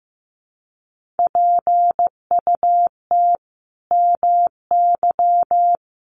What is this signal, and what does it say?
Morse code, heard as a keyed tone, about 15 wpm, 700 Hz. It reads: PUT MY
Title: Sentences from Top 1400 Words 15wpm